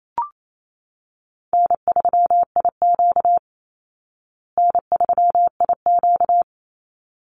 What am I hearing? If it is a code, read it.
Morse code, heard as a keyed tone, about 28 wpm, 700 Hz. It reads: N3IQ N3IQ
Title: Call Signs Repeated 3x - 28wpm